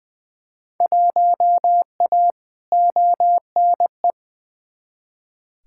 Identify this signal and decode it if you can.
Morse code, heard as a keyed tone, about 20 wpm, 700 Hz. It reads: 1A ONE
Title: Field Day Exchanges Encore - 20wpm